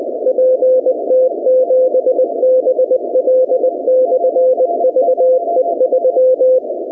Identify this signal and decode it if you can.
Morse code, heard as a keyed tone, about 20 wpm, 500 Hz. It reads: PT7BL/VE3
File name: contests / 2019-cw-wpx-kt5j-k5pi- -2330 0-17